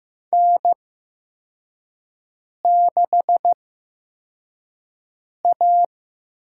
Morse code, heard as a keyed tone, about 15 wpm, spaced as for 4 wpm, 700 Hz. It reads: N6A
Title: USA Call Signs - 8x Character Spacing - 15wpm